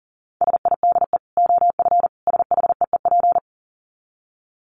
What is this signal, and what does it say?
Morse code, heard as a keyed tone, about 40 wpm, 700 Hz. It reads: SIDE OF SHEEP
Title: Dit Challenge 40wpm